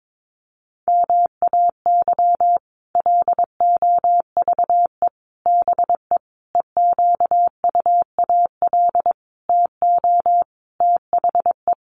Morse code, heard as a keyed tone, about 22 wpm, 700 Hz. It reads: MAY LOVE BE EQUAL TO THE
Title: Sentences from Top 1400 Words 22wpm